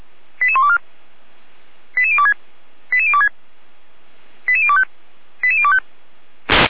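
Five-tone selective call (ZVEI), heard as quick runs of five tones, tones of about 70 ms. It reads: 88124 88026 88026 88025 88025
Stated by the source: ZVEI five-tone selective call received on an SDR.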